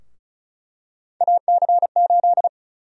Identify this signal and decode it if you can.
Morse code, heard as a keyed tone, about 36 wpm, 700 Hz. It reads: AC8
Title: USA Call Sign Prefixes - 35wpm